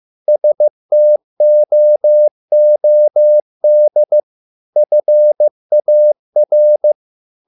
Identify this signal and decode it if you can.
Morse code, heard as a keyed tone, about 15 wpm, 600 Hz. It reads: STOOD FAR